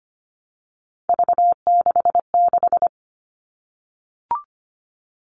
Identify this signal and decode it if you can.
Morse code, heard as a keyed tone, about 25 wpm, 700 Hz. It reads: V66
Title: All Cut Numbers - 3-Digit 25wpm